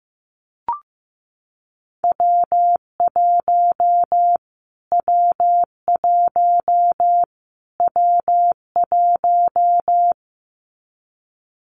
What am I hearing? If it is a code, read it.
Morse code, heard as a keyed tone, about 15 wpm, 700 Hz. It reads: W1 W1 W1
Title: Callsign Prefixes for Top 25 DXCC - Repeated 3x - 15wpm